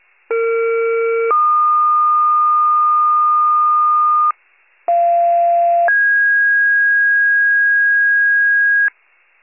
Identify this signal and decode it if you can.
Two-tone page, heard as a one-second tone, then a three-second tone, about 460 then 1220 Hz, about 690 then 1650 Hz, with noise. It is heard from 0.3 to 8.9 s.
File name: wcalert